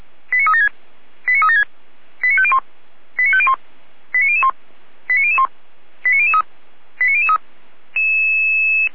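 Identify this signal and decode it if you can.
Five-tone selective call (ZVEI), heard as quick runs of five tones, tones of about 70 ms. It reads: 89376 89376 89501 89501 89001 89001 89003 89003
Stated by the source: ZVEI five-tone selective call for address 89150, received by SDR.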